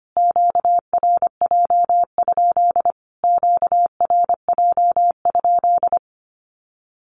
Morse code, heard as a keyed tone, about 25 wpm, 700 Hz. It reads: QRJ? QRJ?
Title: Q-Codes Repeated 3x 25wpm